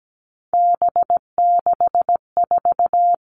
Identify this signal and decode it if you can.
Morse code, heard as a keyed tone, about 17 wpm, 700 Hz. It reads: B64
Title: All Cut Numbers - 3-Digit 17wpm